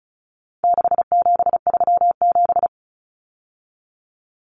Morse code, heard as a keyed tone, about 34 wpm, 700 Hz. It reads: 6737
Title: Common Cut Numbers - 4-Digit 35wpm